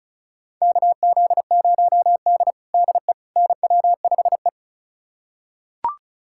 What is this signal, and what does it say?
Morse code, heard as a keyed tone, about 36 wpm, 700 Hz. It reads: KZ0D DE NW5E
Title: USA Callsign DE Callsign - 4 Characters - 35wpm